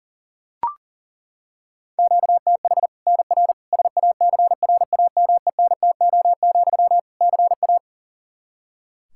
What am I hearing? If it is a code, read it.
Morse code, heard as a keyed tone, about 40 wpm, 700 Hz. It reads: QTH NR SACRAMENTO, CA
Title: Practice QSO - Exchange 1 - Beginning - 40wpm